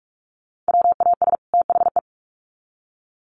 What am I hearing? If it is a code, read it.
Morse code, heard as a keyed tone, about 44 wpm, 700 Hz. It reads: WAS THE
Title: Sets of 2 Words 45wpm